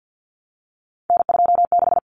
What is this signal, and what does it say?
Morse code, heard as a keyed tone, about 50 wpm, 700 Hz. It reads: N26
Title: Common Cut Numbers - 3-Digit 50wpm